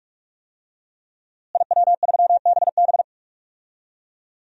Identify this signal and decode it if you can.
Morse code, heard as a keyed tone, about 45 wpm, 700 Hz. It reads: IW3BB
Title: DXCC-100 Call Signs - 5 Characters - Repeated 3x - 45wpm